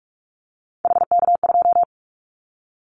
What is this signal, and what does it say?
Morse code, heard as a keyed tone, about 44 wpm, 700 Hz. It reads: HK2
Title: Callsign Prefixes for Top 100 DXCC - 45wpm